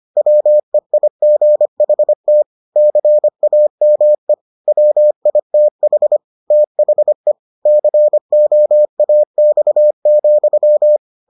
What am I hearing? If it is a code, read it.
Morse code, heard as a keyed tone, about 25 wpm, 600 Hz. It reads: WEIGHT CAME WITH THE COAX,